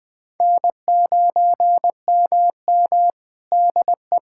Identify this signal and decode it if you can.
Morse code, heard as a keyed tone, about 21 wpm, 700 Hz. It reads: N9MM DE
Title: Practice QSO - Exchange 4 - Ending - 20wpm